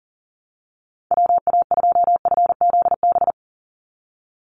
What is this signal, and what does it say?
Morse code, heard as a keyed tone, about 40 wpm, 700 Hz. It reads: WA2FZB
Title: Call Signs Repeated 2x - 40wpm